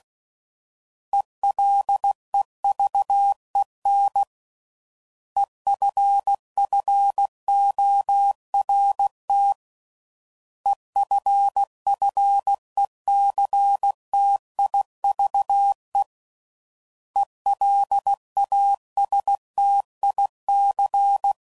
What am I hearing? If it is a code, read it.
Morse code, heard as a keyed tone, about 16 wpm, 800 Hz. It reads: ELEVEN EFFORT EFFECTIVE ELASTIC